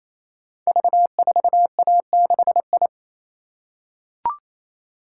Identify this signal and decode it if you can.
Morse code, heard as a keyed tone, about 27 wpm, 700 Hz. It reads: V4A6I
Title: Code Groups - 5 Letter:Numbers - 28wpm